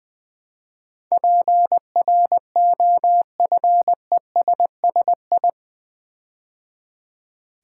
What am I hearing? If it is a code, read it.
Morse code, heard as a keyed tone, about 20 wpm, 700 Hz. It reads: PROFESSI
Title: Build A Word - 4 Letters At A Time - 12-Letter Words - 20wpm